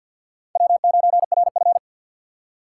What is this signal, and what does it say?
Morse code, heard as a keyed tone, about 47 wpm, 700 Hz. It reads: W9RF
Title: USA Call Signs - 4 Characters - 2x Character Spacing - 50wpm